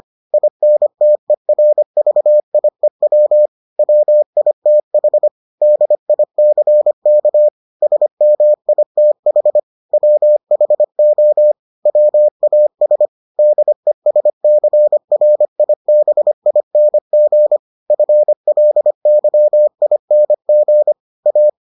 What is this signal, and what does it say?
Morse code, heard as a keyed tone, about 25 wpm, 600 Hz. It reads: INTERVIEW WITH DICK SMITH WHO WAS DESCRIBING FLYING A